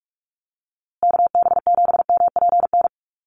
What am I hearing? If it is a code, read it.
Morse code, heard as a keyed tone, about 44 wpm, 700 Hz. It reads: KB7MPN
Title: USA Call Signs Repeated 3x 45wpm